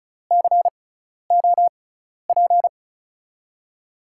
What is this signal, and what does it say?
Morse code, heard as a keyed tone, about 36 wpm, spaced as for 12 wpm, 700 Hz. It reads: COP
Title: USA Call Signs - 5 Characters - 6x Character Spacing - 35wpm